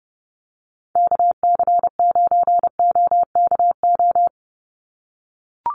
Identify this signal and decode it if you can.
Morse code, heard as a keyed tone, about 30 wpm, 700 Hz. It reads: KC9OKO
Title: USA Call Signs 30wpm